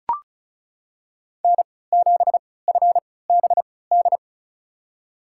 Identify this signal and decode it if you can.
Morse code, heard as a keyed tone, about 36 wpm, spaced as for 20 wpm, 690 Hz. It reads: N7FBD
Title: USA Call Signs - 5 Characters - 3x Character Spacing - 35wpm